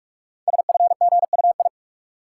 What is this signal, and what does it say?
Morse code, heard as a keyed tone, about 44 wpm, 700 Hz. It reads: SFGUI
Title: Code Groups - 5 Letter:Numbers - 45wpm